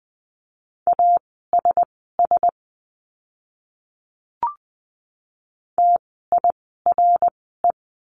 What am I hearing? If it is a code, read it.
Morse code, heard as a keyed tone, about 20 wpm, spaced as for 15 wpm, 700 Hz. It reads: ASS TIRE
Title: 054 Lesson 10f - Letter r - Words 20wpm